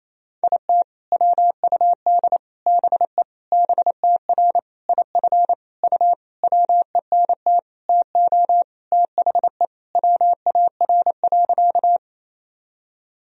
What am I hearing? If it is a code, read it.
Morse code, heard as a keyed tone, about 28 wpm, 700 Hz. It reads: IT WUD BE BTR IF U WENT TO THE WAR.
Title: Sentences from Top 400 Words and Abbreviations 28wpm